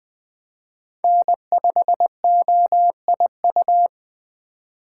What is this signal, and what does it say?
Morse code, heard as a keyed tone, about 24 wpm, 700 Hz. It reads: N5OIU
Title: Speed Racing - USA 1x3 Call Signs 20wpm